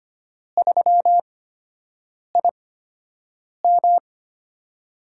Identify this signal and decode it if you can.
Morse code, heard as a keyed tone, about 25 wpm, spaced as for 7 wpm, 700 Hz. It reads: 3IM
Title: USA Call Signs - 4 Characters - 8x Character Spacing - 25wpm